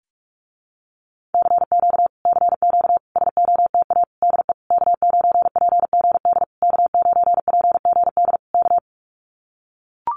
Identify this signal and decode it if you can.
Morse code, heard as a keyed tone, about 44 wpm, 700 Hz. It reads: CQ CQ SOTA DE K9PGD K9PGD K
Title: Common QSO Phrases Encore 45wpm